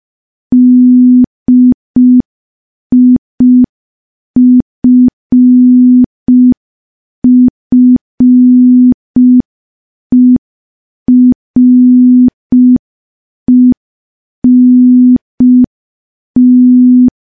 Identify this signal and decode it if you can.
Morse code, heard as a keyed tone, about 5 wpm, 250 Hz. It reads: DIFFERENT
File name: morse / morse-training-05WPM